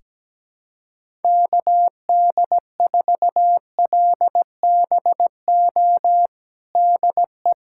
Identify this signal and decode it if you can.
Morse code, heard as a keyed tone, about 17 wpm, 700 Hz. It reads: KD4LBO DE
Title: USA Callsign DE Callsign - 6 Characters - 17wpm